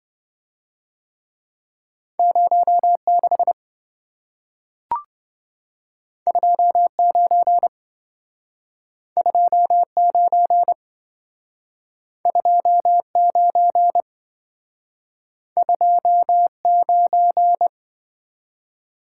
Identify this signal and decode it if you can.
Morse code, heard as a keyed tone, about 26 wpm, 700 Hz. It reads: 06 29 29 29 29